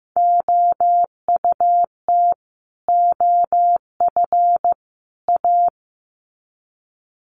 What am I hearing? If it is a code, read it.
Morse code, heard as a keyed tone, about 15 wpm, 700 Hz. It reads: OUT OF A
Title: Sets of 3 Words 15wpm